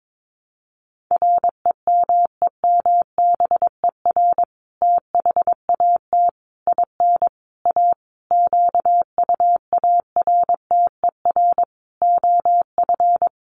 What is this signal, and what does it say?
Morse code, heard as a keyed tone, about 22 wpm, 700 Hz. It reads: REMEMBER THAT IN A QUARTER OF
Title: Sentences from Top 1300 Words 22wpm